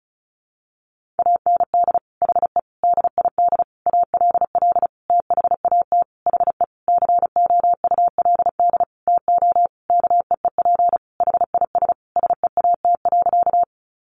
Morse code, heard as a keyed tone, about 34 wpm, 700 Hz. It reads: AND HE DID ALL THAT HE COULD TO KEEP HIS SEAT.
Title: Sentences from Top 1500 Words 35wpm